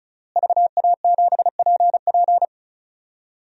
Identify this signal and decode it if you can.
Morse code, heard as a keyed tone, about 36 wpm, 700 Hz. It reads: VA7PP
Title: DXCC-100 Call Signs - 5 Characters - Repeated 3x - 35wpm